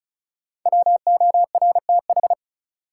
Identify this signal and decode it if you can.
Morse code, heard as a keyed tone, about 34 wpm, 700 Hz. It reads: WORTH
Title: Sentences from Top 800 Words - New Word Drill - 35wpm